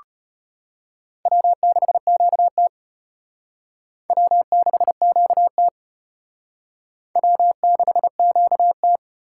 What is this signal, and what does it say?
Morse code, heard as a keyed tone, about 31 wpm, 700 Hz. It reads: W6QT W6QT W6QT
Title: Speed Racing - USA Call Signs 25wpm